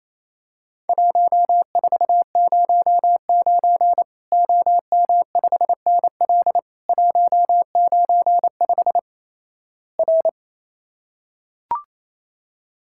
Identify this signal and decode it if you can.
Morse code, heard as a keyed tone, about 28 wpm, 700 Hz. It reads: 1409 OM5NL 195 R
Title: Worked All Europe DX Contest - QTC Exchange - 28wpm